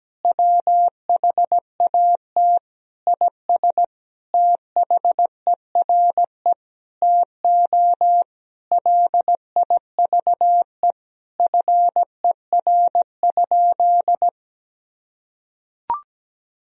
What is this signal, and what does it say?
Morse code, heard as a keyed tone, about 17 wpm, 700 Hz. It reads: WHAT IS THERE TO LIVE FER?
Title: Sentences from Top 200 Words and Abbreviations 17wpm